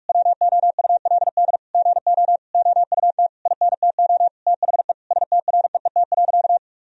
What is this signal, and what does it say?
Morse code, heard as a keyed tone, about 41 wpm, 700 Hz. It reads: WOULD GO OUT INTO THE STREET.
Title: Sentences from Top 1000 Words - Review Part 3 - 45wpm